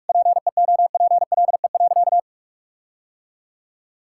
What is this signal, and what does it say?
Morse code, heard as a keyed tone, about 41 wpm, 700 Hz. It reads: PEOPLE.
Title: Sentences from Top 1600 Words 45wpm